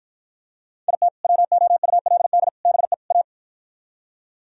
Morse code, heard as a keyed tone, about 53 wpm, 700 Hz. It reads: IT WOULD BE A